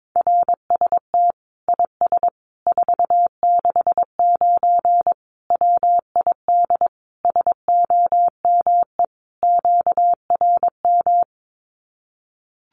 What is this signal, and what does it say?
Morse code, heard as a keyed tone, about 22 wpm, 700 Hz. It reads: RST IS 469 WID SOME QRM
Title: Common QSO Phrases 22wpm